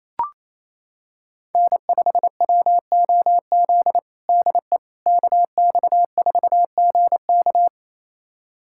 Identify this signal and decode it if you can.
Morse code, heard as a keyed tone, about 28 wpm, 700 Hz. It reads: N5WOZ DE KX4GK
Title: USA Callsign DE Callsign - 5 Characters - 28wpm